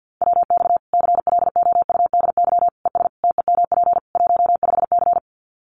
Morse code, heard as a keyed tone, about 48 wpm, 700 Hz. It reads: WX CLOUDY ES TEMP 15C
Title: Common QSO Phrases 50wpm